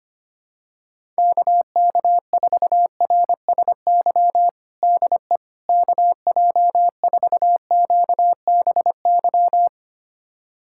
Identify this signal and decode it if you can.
Morse code, heard as a keyed tone, about 25 wpm, 700 Hz. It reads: KK4RSY DE KJ4QBY
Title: USA Callsign DE Callsign - 6 Characters - 25wpm